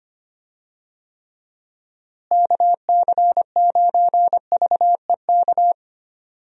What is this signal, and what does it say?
Morse code, heard as a keyed tone, about 25 wpm, 700 Hz. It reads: KC9VEK
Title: USA Call Signs - 6 Characters - Repeated 3x - 25wpm